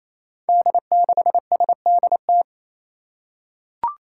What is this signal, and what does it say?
Morse code, heard as a keyed tone, about 28 wpm, 700 Hz. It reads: D6SDT
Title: Five Challenging Letter-Number Combinations 28wpm